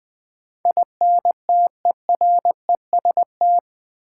Morse code, heard as a keyed tone, about 21 wpm, 700 Hz. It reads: INTEREST
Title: Sets of 5 Words Encore - New Word Drill - 20wpm